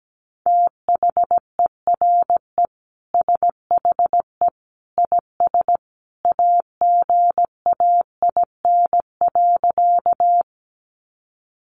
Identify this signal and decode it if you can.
Morse code, heard as a keyed tone, about 17 wpm, 700 Hz. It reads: THERE SHE IS AGAIN.
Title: Sentences from Top 200 Words 17wpm